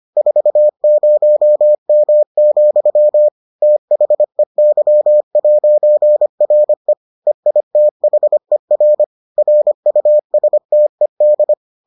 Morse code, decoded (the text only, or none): 40M, THEY'RE EITHER RUSTED